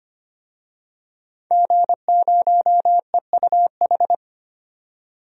Morse code, heard as a keyed tone, about 25 wpm, 700 Hz. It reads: G0EUH
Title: DXCC-100 Call Signs - 5 Characters - 25wpm